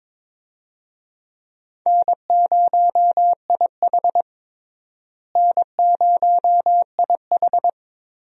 Morse code, heard as a keyed tone, about 22 wpm, 700 Hz. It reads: N0IH N0IH
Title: USA Call Signs - 4 Characters - Repeated 2x - 22wpm